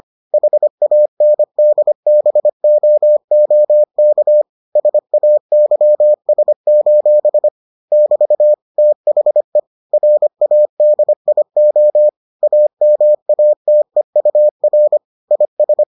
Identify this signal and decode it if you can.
Morse code, heard as a keyed tone, about 25 wpm, 600 Hz. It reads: HANDBOOK SAYS: <BT> THE RADIO AMATEUR IS